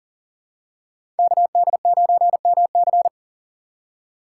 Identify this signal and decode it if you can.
Morse code, heard as a keyed tone, about 40 wpm, 700 Hz. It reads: KD9MC
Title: USA Call Signs - 5 Characters - 2x Character Spacing - 40wpm